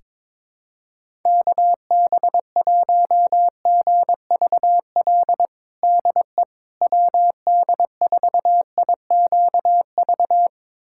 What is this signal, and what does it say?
Morse code, heard as a keyed tone, about 22 wpm, 700 Hz. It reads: KB1GVL DE WD4IQV
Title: USA Callsign DE Callsign - 6 Characters - 22wpm